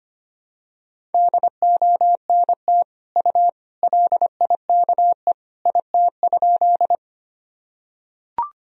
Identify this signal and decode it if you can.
Morse code, heard as a keyed tone, about 25 wpm, 700 Hz. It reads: DONT U LIKE IT?
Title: Sentences from Top 300 Words and Abbreviations 25wpm